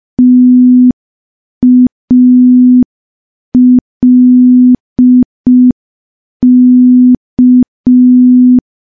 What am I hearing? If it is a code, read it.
Morse code, heard as a keyed tone, about 5 wpm, 250 Hz. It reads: TALK
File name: morse / morse-training-05WPM